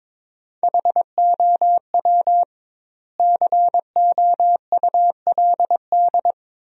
Morse code, heard as a keyed tone, about 22 wpm, 700 Hz. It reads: HOW COULD
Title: 2x Word-Spacing - Sentences from Top 200 Words - 22wpm